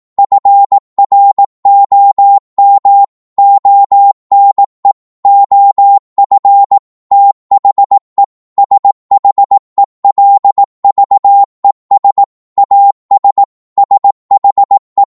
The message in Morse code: FROM ONE OF THE SHELVES AS SHE